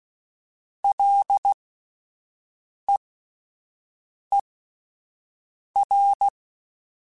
Morse code, heard as a keyed tone, about 16 wpm, spaced as for 5 wpm, 800 Hz. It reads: LEER
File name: Alphabetical / Alphabetical mp3s / Ls @5